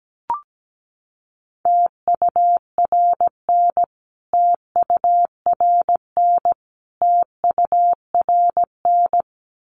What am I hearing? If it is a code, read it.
Morse code, heard as a keyed tone, about 17 wpm, 700 Hz. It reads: TURN TURN TURN
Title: Top 300 Words Repeated 17wpm